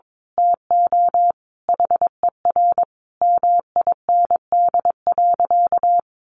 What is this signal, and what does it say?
Morse code, heard as a keyed tone, about 22 wpm, 700 Hz. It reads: TO HER MIND.
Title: Sentences from Top 1300 Words 22wpm